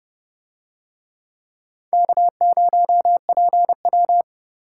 Morse code, heard as a keyed tone, about 29 wpm, 700 Hz. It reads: K0PW